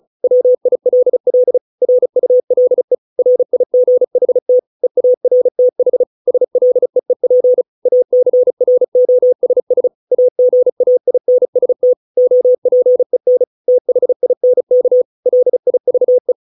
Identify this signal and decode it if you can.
Morse code, heard as a keyed tone, about 34 wpm, 500 Hz. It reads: WILL RULE RIGHT EARTH SLEEP ACROSS AGAINST OPEN THINK LIVE